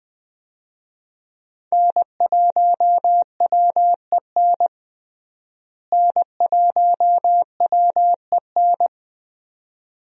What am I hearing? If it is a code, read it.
Morse code, heard as a keyed tone, about 20 wpm, 700 Hz. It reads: N1WEN N1WEN
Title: USA Call Signs - 5 Characters - Repeated 2x - 20wpm